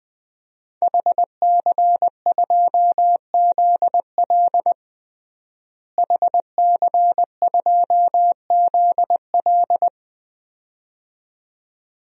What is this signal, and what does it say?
Morse code, heard as a keyed tone, about 21 wpm, 700 Hz. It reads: HC2ZL HC2ZL
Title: DXCC-100 Call Signs - 5 Characters - Repeated 2x - 20wpm